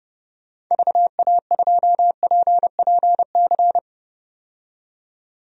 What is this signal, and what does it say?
Morse code, heard as a keyed tone, about 32 wpm, 700 Hz. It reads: VA2PPC
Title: DXCC-100 Call Signs - 6 Characters - 30wpm